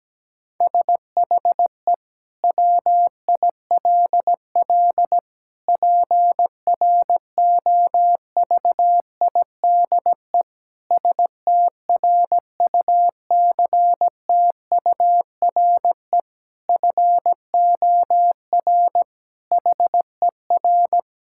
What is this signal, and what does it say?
Morse code, heard as a keyed tone, about 17 wpm, 700 Hz. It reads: SHE WILL PROVIDE STRUCTURE FOR HER
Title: Sentences from Top 1000 Words - Review Part 6 - 17wpm